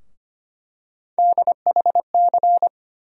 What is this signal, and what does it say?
Morse code, heard as a keyed tone, about 25 wpm, 700 Hz. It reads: DHC
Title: USA Call Sign Postfixes - 3 Letters 25wpm